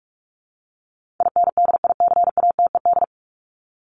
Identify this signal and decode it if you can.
Morse code, heard as a keyed tone, about 44 wpm, 700 Hz. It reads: INDICATED
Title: ED Suffix - Top 100 Words with Base Words 45wpm